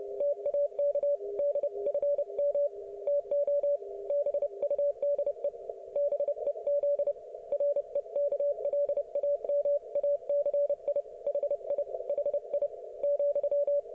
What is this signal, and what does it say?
Morse code, heard as a keyed tone, about 30 wpm, 570 Hz. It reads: TAK DFM TO BUDE BEZ REKLAMACI HI HI,
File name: ok1fhi